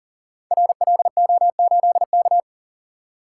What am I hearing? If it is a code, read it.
Morse code, heard as a keyed tone, about 41 wpm, 690 Hz. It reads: RLO8K
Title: Code Groups - 5 Letter:Numbers - 40wpm